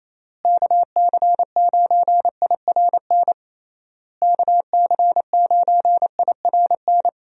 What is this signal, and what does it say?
Morse code, heard as a keyed tone, about 28 wpm, 700 Hz. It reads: KC9IRN KC9IRN
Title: USA Call Signs - 6 Characters - Repeated 3x - 28wpm